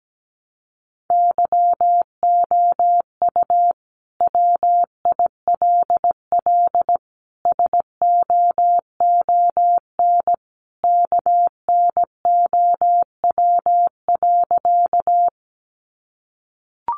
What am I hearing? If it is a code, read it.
Morse code, heard as a keyed tone, about 17 wpm, 700 Hz. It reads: YOU WILL SOON KNOW.
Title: Sentences from Top 400 Words 17wpm